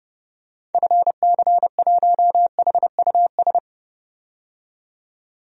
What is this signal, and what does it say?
Morse code, heard as a keyed tone, about 30 wpm, 700 Hz. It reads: FC1HUS
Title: DXCC-100 Call Signs - 6 Characters - 30wpm